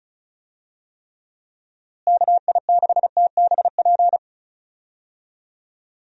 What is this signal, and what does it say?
Morse code, heard as a keyed tone, about 36 wpm, 700 Hz. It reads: KI6TBP
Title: USA Call Signs - 6 Characters - Repeated 2x - 35wpm